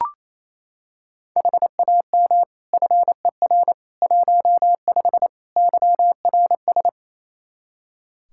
Morse code, heard as a keyed tone, about 27 wpm, 700 Hz. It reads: HAM FER 15 YRS
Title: K1USN SST QSO Practice 28wpm